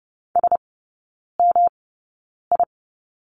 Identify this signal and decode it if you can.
Morse code, heard as a keyed tone, about 30 wpm, spaced as for 9 wpm, 700 Hz. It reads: SMI
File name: Lesson 18e - Letter m - Three Characters 30wpm